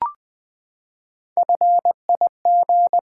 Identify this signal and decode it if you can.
Morse code, heard as a keyed tone, about 20 wpm, 700 Hz. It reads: FIG
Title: Build A Word - 3 Letters At A Time - 8-Letter Words - 20wpm